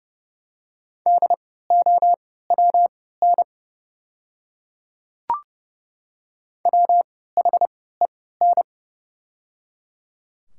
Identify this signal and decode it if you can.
Morse code, heard as a keyed tone, about 30 wpm, spaced as for 17 wpm, 700 Hz. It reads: DOWN WHEN
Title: 3x Character Spacing - Top 200 Words - 30wpm